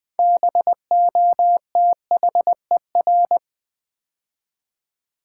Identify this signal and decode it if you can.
Morse code, heard as a keyed tone, about 20 wpm, 700 Hz. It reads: BOTHER
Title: Build A Word - 2 Letters At A Time - 6-Letter Words - 20wpm